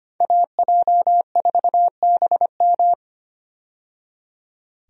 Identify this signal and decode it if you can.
Morse code, heard as a keyed tone, about 25 wpm, 700 Hz. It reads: AJ4BM
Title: USA Call Signs - 5 Characters - 25wpm